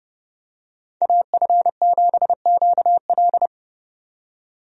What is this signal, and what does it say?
Morse code, heard as a keyed tone, about 32 wpm, 700 Hz. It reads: AF7QL
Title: USA Call Signs - 5 Characters - 30wpm